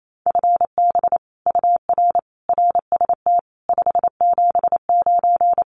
Morse code, heard as a keyed tone, about 28 wpm, 700 Hz. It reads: FB UR RST 579
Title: Common QSO Phrases Encore 28wpm